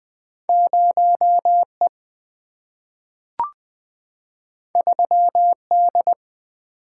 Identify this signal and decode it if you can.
Morse code, heard as a keyed tone, about 20 wpm, 700 Hz. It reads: 0E 3D
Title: Single-Letter Single-Number 20wpm